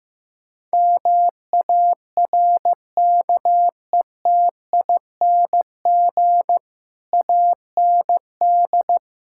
Morse code, heard as a keyed tone, about 15 wpm, 700 Hz. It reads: MARKETING AND
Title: Sentences from Top 1000 Words - Review Part 5 - 15wpm